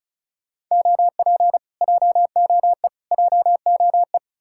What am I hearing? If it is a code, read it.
Morse code, heard as a keyed tone, about 46 wpm, 700 Hz. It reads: OP JOE JOE